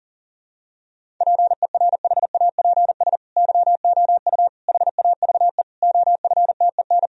Morse code, heard as a keyed tone, about 40 wpm, 700 Hz. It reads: PERHAPS YOU HAVE OFTEN
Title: Sentences from Top 1000 Words - Review Part 1 - 40wpm